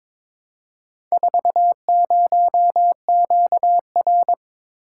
Morse code, heard as a keyed tone, about 22 wpm, 700 Hz. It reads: 40QR
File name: practice / 22wpm / Four Letter-Number Combinations - 22wpm